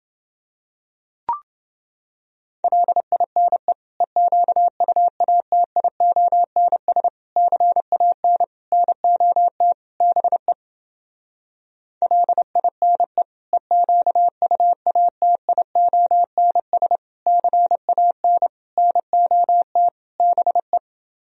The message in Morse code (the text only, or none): LINE EQUATIONS CAN NOT BE LINE EQUATIONS CAN NOT BE